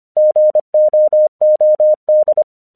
Morse code, heard as a keyed tone, about 25 wpm, 610 Hz. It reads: GOOD
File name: 1 English 25wpm / AQC11